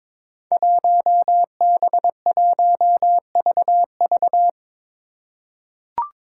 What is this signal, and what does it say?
Morse code, heard as a keyed tone, about 22 wpm, 700 Hz. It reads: 1B1VV
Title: Five Challenging Letter-Number Combinations 22wpm